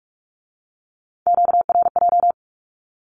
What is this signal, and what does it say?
Morse code, heard as a keyed tone, about 46 wpm, 700 Hz. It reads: QRJ
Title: Q-Codes with Spoken Meaning - 45wpm